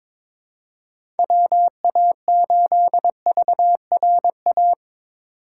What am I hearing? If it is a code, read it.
Morse code, heard as a keyed tone, about 22 wpm, 700 Hz. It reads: WA8VRA
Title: USA Call Signs - 4x Character Spacing - 22wpm